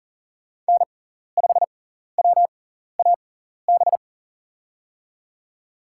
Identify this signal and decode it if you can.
Morse code, heard as a keyed tone, about 40 wpm, spaced as for 14 wpm, 700 Hz. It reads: N5WAB
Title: USA Call Signs - 5 Characters - 6x Character Spacing - 40wpm